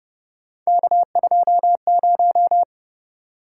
Morse code, heard as a keyed tone, about 30 wpm, 700 Hz. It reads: K20
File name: Speed-Racing - Three Letter-Number Combinations - 25wpm